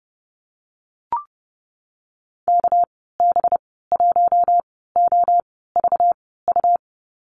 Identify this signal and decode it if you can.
Morse code, heard as a keyed tone, about 30 wpm, spaced as for 17 wpm, 700 Hz. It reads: KB1OVU
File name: Lesson 37g - Letter x - Call Signs 30wpm